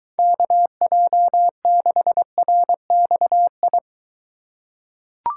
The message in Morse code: KJ6RXI

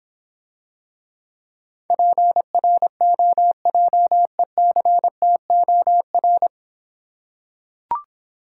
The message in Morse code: PROJECTOR

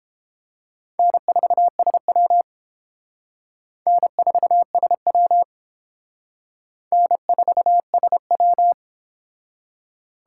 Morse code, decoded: N4SW N4SW N4SW

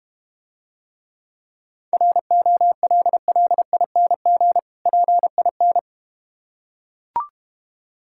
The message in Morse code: ROLLING PIN

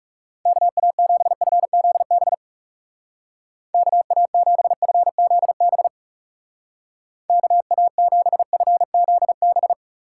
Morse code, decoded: KA7FZB KA7FZB KA7FZB

